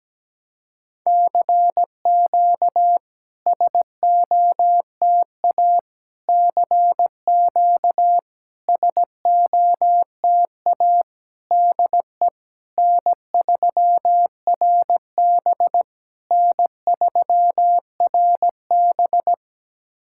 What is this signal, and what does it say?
Morse code, heard as a keyed tone, about 17 wpm, 700 Hz. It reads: CQ SOTA CQ SOTA DE N3RB N3RB
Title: SOTA QSO Practice 17wpm